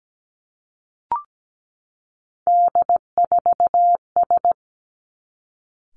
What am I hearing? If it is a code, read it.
Morse code, heard as a keyed tone, about 17 wpm, 700 Hz. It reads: D4S
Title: Three Challenging Letter-Number Combinations 17wpm